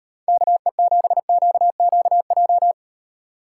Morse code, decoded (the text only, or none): KE7QQJ